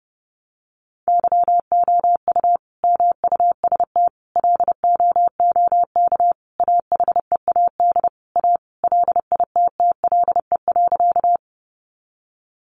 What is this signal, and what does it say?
Morse code, heard as a keyed tone, about 30 wpm, 700 Hz. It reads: YOU MUST LOOK AHEAD A LITTLE.
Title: Sentences from Top 1100 Words 30wpm